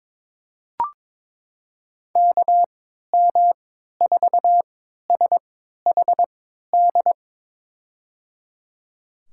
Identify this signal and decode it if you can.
Morse code, heard as a keyed tone, about 22 wpm, spaced as for 13 wpm, 700 Hz. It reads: KM4SHD
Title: USA Call Signs - 3x Character Spacing - 22wpm